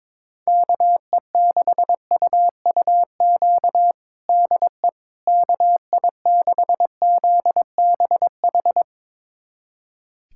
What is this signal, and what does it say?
Morse code, heard as a keyed tone, about 21 wpm, 700 Hz. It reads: KE6UUQ DE KI6ZBH
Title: USA Callsign DE Callsign 22wpm